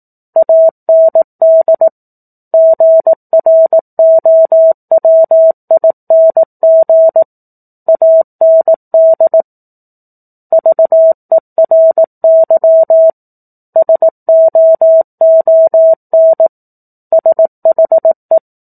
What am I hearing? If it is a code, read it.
Morse code, heard as a keyed tone, about 18 wpm, 650 Hz. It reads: AND GROWING AND VERY SOON SHE
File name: 04-Chapter-04 18 WPM